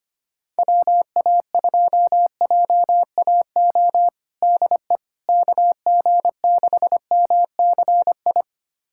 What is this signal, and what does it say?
Morse code, heard as a keyed tone, about 25 wpm, 700 Hz. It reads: WA2JAO DE KG6MCI
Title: USA Callsign DE Callsign 25wpm